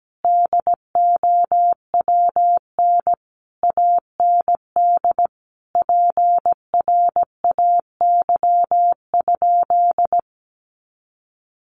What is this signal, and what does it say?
Morse code, heard as a keyed tone, about 17 wpm, 700 Hz. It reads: DOWN AND PRAY?
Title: A Sound Of Thunder 17wpm